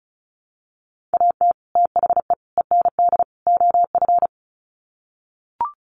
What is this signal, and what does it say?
Morse code, heard as a keyed tone, about 34 wpm, 700 Hz. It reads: AT THE END OF
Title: Sets of 4 Words 35wpm